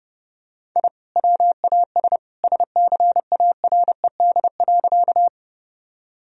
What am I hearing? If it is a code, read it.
Morse code, heard as a keyed tone, about 30 wpm, 700 Hz. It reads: I WAS SCARED.
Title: Sentences from Top 1000 Words - Review Part 4 - 30wpm